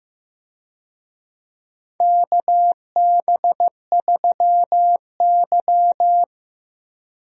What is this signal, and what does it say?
Morse code, heard as a keyed tone, about 15 wpm, 700 Hz. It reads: KB3Y
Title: USA Call Signs - 4 Characters - Repeated 3x - 15wpm